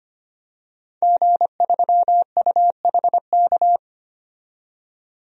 Morse code, heard as a keyed tone, about 25 wpm, 700 Hz. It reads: G3UHK
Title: DXCC-100 Call Signs - 5 Characters - 25wpm